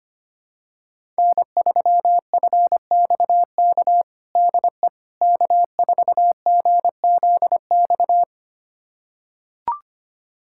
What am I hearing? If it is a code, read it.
Morse code, heard as a keyed tone, about 25 wpm, 700 Hz. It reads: N3FXK DE K4GZX
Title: USA Callsign DE Callsign - 5 Characters - 25wpm